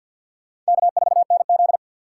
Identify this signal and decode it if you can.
Morse code, heard as a keyed tone, about 60 wpm, 700 Hz. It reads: K4NZ
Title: Speed Racing - USA 1x2 Call Signs 50wpm